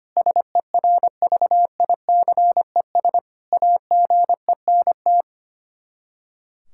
Morse code, heard as a keyed tone, about 25 wpm, 700 Hz. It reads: SERVICES AGENT
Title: Practice QSO - Exchange 3 - More Info - 25wpm